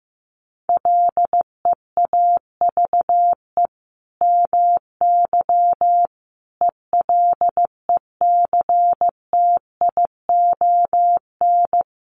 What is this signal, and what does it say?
Morse code, heard as a keyed tone, about 15 wpm, 700 Hz. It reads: LEAVE MY ELECTION
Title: Sentences from Top 1500 Words 15wpm